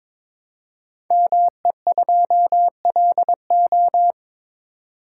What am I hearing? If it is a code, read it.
Morse code, heard as a keyed tone, about 22 wpm, 700 Hz. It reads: ME2LO
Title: Code Groups - 5 Letter:Numbers - 22wpm